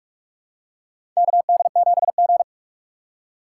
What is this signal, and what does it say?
Morse code, heard as a keyed tone, about 46 wpm, 700 Hz. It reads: KD7G
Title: USA Call Signs - 4 Characters - 3x Character Spacing - 45wpm